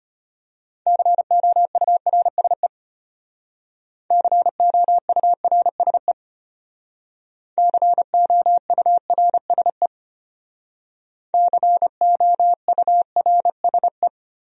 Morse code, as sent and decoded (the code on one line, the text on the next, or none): -.-. --- ..- .-. ... / -.-. --- ..- .-. ... / -.-. --- ..- .-. ... / -.-. --- ..- .-. ... .
COURSE COURSE COURSE COURSE